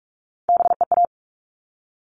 Morse code, heard as a keyed tone, about 44 wpm, 700 Hz. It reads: BEA
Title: All Cut Numbers - 3-Digit 45wpm